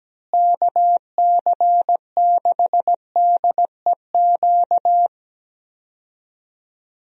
Morse code, heard as a keyed tone, about 17 wpm, 700 Hz. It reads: KC6DEQ
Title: USA Call Signs - 6 Characters - Repeated 2x - 17wpm